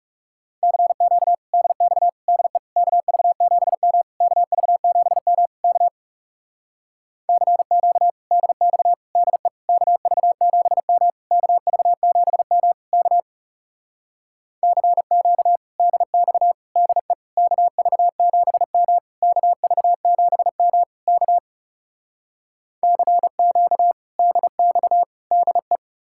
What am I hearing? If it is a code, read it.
Morse code, heard as a keyed tone, about 40 wpm, 700 Hz. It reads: CQ DX DE KV7M KV7M K CQ DX DE KV7M KV7M K CQ DX DE KV7M KV7M K CQ DX DE